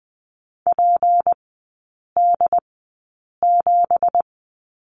Morse code, heard as a keyed tone, about 20 wpm, spaced as for 8 wpm, 690 Hz. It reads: PD7
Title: 212 Lesson 34e - Number 8 - Three Characters 20wpm